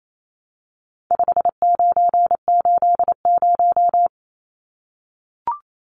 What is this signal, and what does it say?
Morse code, heard as a keyed tone, about 28 wpm, 700 Hz. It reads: 5980
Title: Numbers - 4-Digit 28wpm